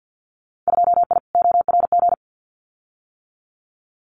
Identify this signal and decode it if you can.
Morse code, heard as a keyed tone, about 50 wpm, 700 Hz. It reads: 2I ORG
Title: Winter Field Day Exchanges Encore 50wpm